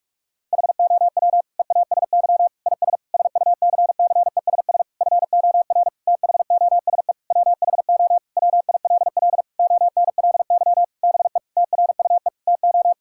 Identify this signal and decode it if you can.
Morse code, heard as a keyed tone, about 46 wpm, 700 Hz. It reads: HOW EASY IS SUCCESS FOR THOSE WHO WILL ONLY BE TRUE TO
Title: Sentences from Top 1000 Words - Review Part 6 - 45wpm